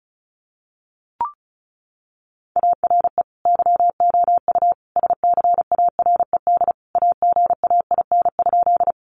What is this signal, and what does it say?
Morse code, heard as a keyed tone, about 34 wpm, 700 Hz. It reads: ARE YOU SCARED AGAIN?
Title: Sentences from Top 700 Words 35wpm